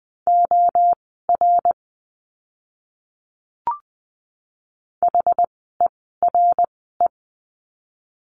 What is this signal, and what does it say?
Morse code, heard as a keyed tone, about 20 wpm, spaced as for 14 wpm, 700 Hz. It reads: OR HERE
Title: 139 Lesson 23f - Letter f - Words 20wpm